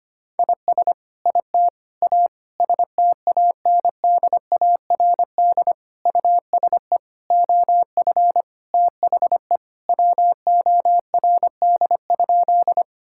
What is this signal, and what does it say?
Morse code, heard as a keyed tone, about 25 wpm, 700 Hz. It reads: IS IT A STANDARD USE OF THE WORD?
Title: Sentences from Top 1000 Words - Review Part 6 - 25wpm